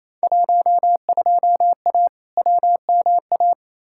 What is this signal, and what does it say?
Morse code, heard as a keyed tone, about 28 wpm, 700 Hz. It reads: 12A WMA
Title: Field Day Exchanges Encore - 28wpm